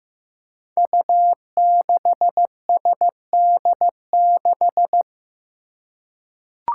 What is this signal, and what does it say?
Morse code, heard as a keyed tone, about 15 wpm, 700 Hz. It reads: U6SD6
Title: Five Challenging Letter-Number Combinations 15wpm